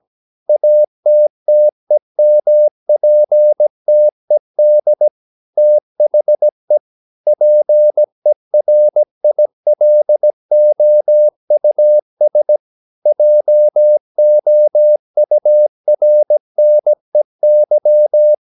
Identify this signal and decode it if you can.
Morse code, heard as a keyed tone, about 17 wpm, 600 Hz. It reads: ATTEMPTED THE PERILOUS JOURNEY
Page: Romanen Never af Ken Follet på engelsk.